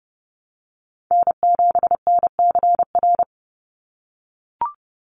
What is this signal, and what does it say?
Morse code, heard as a keyed tone, about 30 wpm, 700 Hz. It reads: N7NCR
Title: USA Call Signs 30wpm